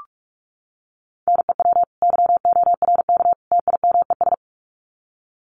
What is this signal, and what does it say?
Morse code, heard as a keyed tone, about 44 wpm, 700 Hz. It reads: NEW YORK TIMES
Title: Sets of 3 Words 45wpm